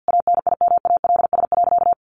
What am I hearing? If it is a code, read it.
Morse code, heard as a keyed tone, about 50 wpm, 700 Hz. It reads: ANIMALS.
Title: A Sound Of Thunder 50wpm